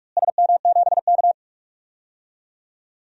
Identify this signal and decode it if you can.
Morse code, heard as a keyed tone, about 47 wpm, 700 Hz. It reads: SM7K
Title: DXCC-100 Call Signs - 4 Characters - Repeated 2x - 45wpm